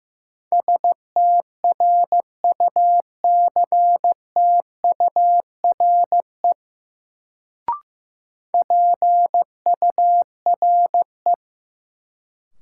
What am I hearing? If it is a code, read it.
Morse code, heard as a keyed tone, about 16 wpm, 700 Hz. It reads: STRUCTURE PURE
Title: Sentences from Top 1500 Words - New Word Drill Alt - 15wpm